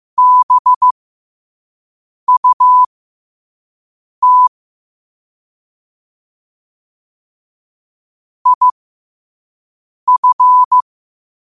Morse code, heard as a keyed tone, about 15 wpm, spaced as for 5 wpm, 1000 Hz. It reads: BUT IF
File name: qotdfiles / 5wpm